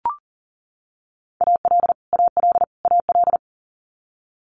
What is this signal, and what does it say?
Morse code, heard as a keyed tone, about 38 wpm, 700 Hz. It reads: AL AL AL
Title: Multi-letter Phonemes Repeated 3x 40wpm